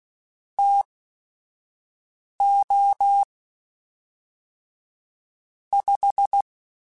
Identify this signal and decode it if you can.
Morse code, heard as a keyed tone, about 16 wpm, spaced as for 5 wpm, 800 Hz. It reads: TO 5